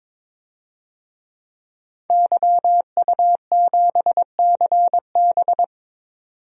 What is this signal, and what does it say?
Morse code, heard as a keyed tone, about 21 wpm, 700 Hz. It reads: YU7CB
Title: DXCC-100 Call Signs - 5 Characters - 22wpm